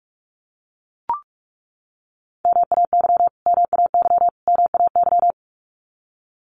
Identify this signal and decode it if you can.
Morse code, heard as a keyed tone, about 44 wpm, 700 Hz. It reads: MAY MAY MAY
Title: Top 500 Words Repeated 45wpm